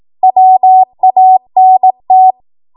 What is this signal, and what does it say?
Morse code, heard as a keyed tone, about 18 wpm, 750 Hz. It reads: WANT
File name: Archive / 18 WPM / 18WPM